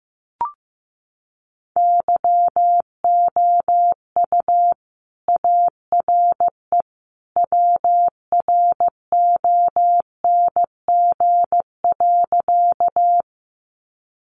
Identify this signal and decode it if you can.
Morse code, heard as a keyed tone, about 15 wpm, 700 Hz. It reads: YOU ARE WRONG.
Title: Sentences from Top 900 Words 15wpm